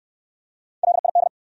Morse code, heard as a keyed tone, about 60 wpm, 700 Hz. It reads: LID